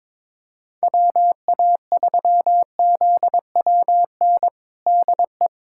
Morse code, heard as a keyed tone, about 22 wpm, 700 Hz. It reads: WA3ZWN DE
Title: USA Callsign DE Callsign 22wpm